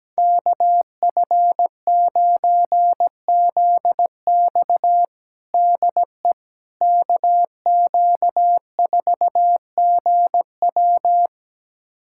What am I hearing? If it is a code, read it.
Morse code, heard as a keyed tone, about 17 wpm, 700 Hz. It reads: KF9ZX DE KQ4GW
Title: USA Callsign DE Callsign - 5 Characters - 17wpm